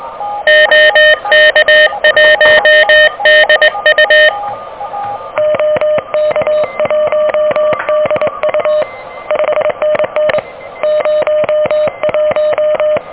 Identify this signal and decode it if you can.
Morse code, heard as a keyed tone, about 23 wpm, 610 Hz, with noise. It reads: OK1DU OK1DU 5NN 01
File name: OK QRP 2023 / OK QRP 2023